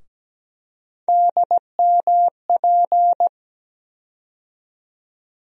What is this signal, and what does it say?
Morse code, heard as a keyed tone, about 17 wpm, 700 Hz. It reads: DMP
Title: USA Call Sign Postfixes - 3 Letters 17wpm